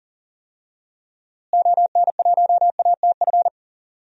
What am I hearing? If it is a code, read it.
Morse code, heard as a keyed tone, about 37 wpm, 700 Hz. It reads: ON1ATF
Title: DXCC-100 Call Signs - 6 Characters - Repeated 3x - 40wpm